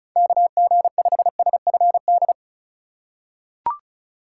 Morse code, heard as a keyed tone, about 36 wpm, 700 Hz. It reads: KG5SFD
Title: USA Call Signs - 6 Characters - 2x Character Spacing - 35wpm